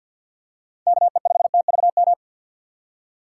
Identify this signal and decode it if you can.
Morse code, heard as a keyed tone, about 51 wpm, 700 Hz. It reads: KE5TVM
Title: USA Call Signs - 6 Characters - 4x Character Spacing - 50wpm